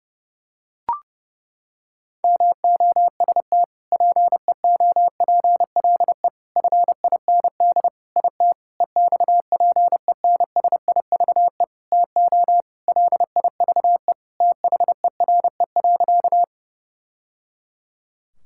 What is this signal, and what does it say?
Morse code, heard as a keyed tone, about 30 wpm, 700 Hz. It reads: MOST PEOPLE FIND IT EXPENSIVE TO LIVE THERE.
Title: Sentences from Top 1000 Words - Review Part 6 - 30wpm